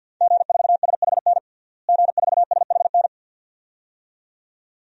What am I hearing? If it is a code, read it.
Morse code, heard as a keyed tone, about 47 wpm, 700 Hz. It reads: G4SHN G4SHN
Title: DXCC-100 Call Signs - 5 Characters - Repeated 2x - 50wpm